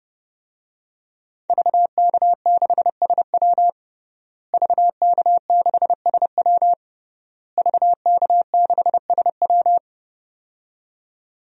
Morse code, heard as a keyed tone, about 30 wpm, 700 Hz. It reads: VK6SW VK6SW VK6SW
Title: DXCC-100 Call Signs - 5 Characters - Repeated 3x - 30wpm